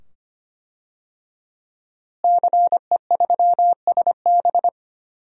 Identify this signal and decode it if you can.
Morse code, heard as a keyed tone, about 25 wpm, 700 Hz. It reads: CE3SB
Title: DXCC-100 Call Signs - 5 Characters - Repeated 2x - 25wpm